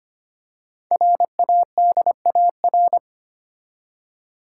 Morse code, heard as a keyed tone, about 25 wpm, 700 Hz. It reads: RADAR
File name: 5 Letter Words - First 3 Letters Given - 25wpm